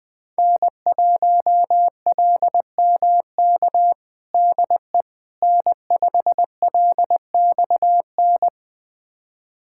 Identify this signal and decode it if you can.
Morse code, heard as a keyed tone, about 21 wpm, 700 Hz. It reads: N1LMK DE N5LXN
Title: USA Callsign DE Callsign - 5 Characters - 20wpm